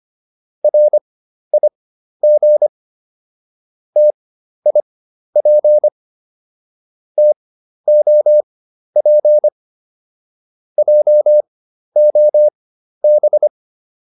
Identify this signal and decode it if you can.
Morse code, heard as a keyed tone, about 25 wpm, spaced as for 12 wpm, 600 Hz. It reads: RIG TIP TOP JOB